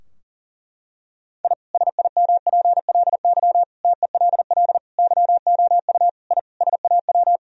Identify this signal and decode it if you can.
Morse code, heard as a keyed tone, about 40 wpm, 700 Hz. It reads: I SIMPLY TELL YOU I SAW
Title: Sentences from Top 1000 Words - Review Part 4 - 40wpm